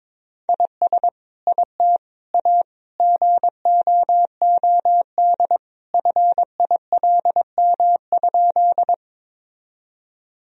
Morse code, heard as a keyed tone, about 22 wpm, 700 Hz. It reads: IS IT A GOOD FILM?
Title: Sentences from Top 1000 Words - Review Part 3 - 22wpm